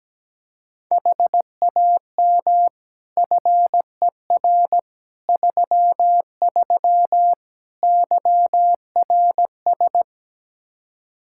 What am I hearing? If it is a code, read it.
Morse code, heard as a keyed tone, about 17 wpm, 700 Hz. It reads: HAM FER 33 YRS
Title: Practice QSO - Exchange 3 - More Info - 17wpm